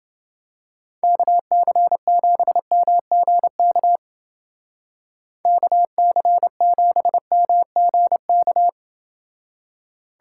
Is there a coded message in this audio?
Yes: Morse code from 1.0 to 8.7 s, heard as a keyed tone, about 30 wpm, 700 Hz. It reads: KC7MGK KC7MGK